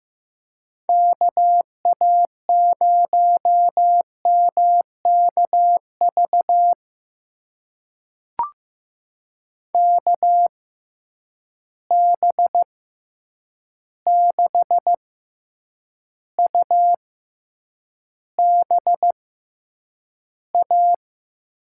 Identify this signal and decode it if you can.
Morse code, heard as a keyed tone, about 15 wpm, spaced as for 8 wpm, 700 Hz. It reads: KA0MKV KB6UBA
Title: USA Call Signs - 6 Characters - 6x Character Spacing - 15wpm